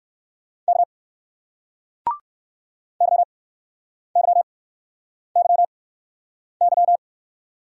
Morse code, heard as a keyed tone, about 58 wpm, spaced as for 9 wpm, 700 Hz. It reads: K YYYY